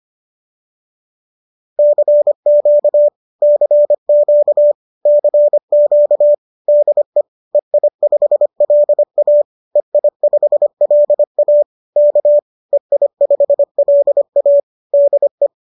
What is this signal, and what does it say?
Morse code, heard as a keyed tone, about 25 wpm, 590 Hz. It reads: CQ CQ CQ DE EI5LA EI5LA K EI5LA DE